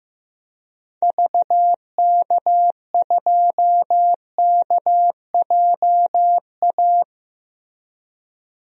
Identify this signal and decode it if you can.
Morse code, heard as a keyed tone, about 15 wpm, 700 Hz. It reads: VK2KJA
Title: DXCC-100 Call Signs - 6 Characters - 15wpm